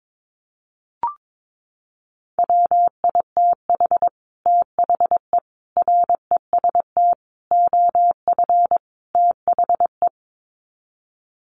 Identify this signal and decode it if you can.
Morse code, heard as a keyed tone, about 22 wpm, 700 Hz. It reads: WITH THE REST OF THE
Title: Sets of 5 Words 22wpm